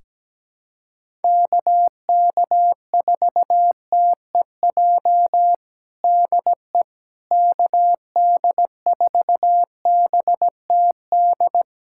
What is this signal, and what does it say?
Morse code, heard as a keyed tone, about 17 wpm, 700 Hz. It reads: KK4TEJ DE KD4BTD
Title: USA Callsign DE Callsign - 6 Characters - 17wpm